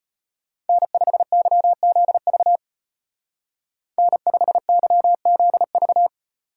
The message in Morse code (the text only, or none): N5YZV N5YZV